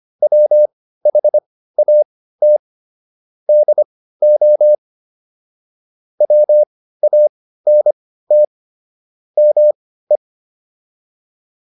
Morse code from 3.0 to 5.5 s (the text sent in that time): DO